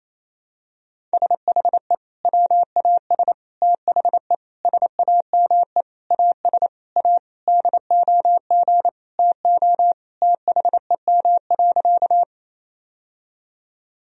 Morse code, heard as a keyed tone, about 27 wpm, 700 Hz. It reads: SHE WAS THE SAME AS A DOG TO THEM.
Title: Sentences from Top 1000 Words - Review Part 2 - 28wpm